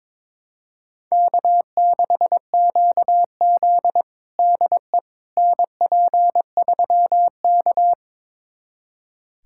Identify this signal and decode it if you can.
Morse code, heard as a keyed tone, about 22 wpm, 700 Hz. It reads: K6QZ DE NP3K
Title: USA Callsign DE Callsign - 4 Characters - 22wpm